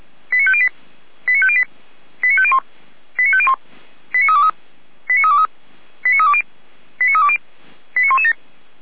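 Five-tone selective call (ZVEI), heard as quick runs of five tones, tones of about 70 ms. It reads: 89508 89508 89501 89501 89323 89323 89320 89320 89117
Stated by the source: ZVEI five-tone selective call received on an SDR.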